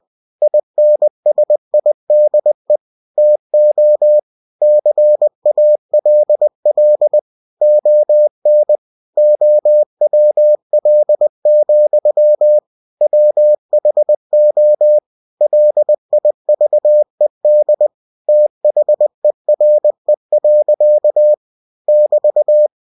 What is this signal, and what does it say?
Morse code, heard as a keyed tone, about 20 wpm, 600 Hz. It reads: INSIDE TO CALL ON OWL, WHO LIVED THERE. <BT>
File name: cw-books / winnie-the-pooh / winnie-20wpm-page-0050